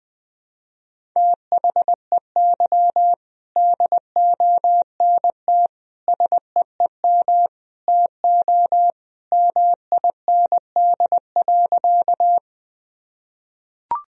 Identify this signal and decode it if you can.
Morse code, heard as a keyed tone, about 20 wpm, 700 Hz. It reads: THEY DONT SEEM TO MIND.
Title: Sentences from Top 1000 Words - Review Part 1 - 20wpm